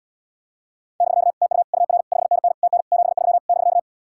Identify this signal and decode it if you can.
Morse code, heard as a keyed tone, about 78 wpm, 700 Hz. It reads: <BK> TU UR 5NN TN 73 <BK>